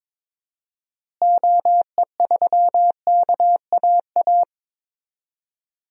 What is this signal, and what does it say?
Morse code, heard as a keyed tone, about 22 wpm, 700 Hz. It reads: OE3KAA
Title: DXCC-100 Call Signs - 6 Characters - Repeated 2x - 22wpm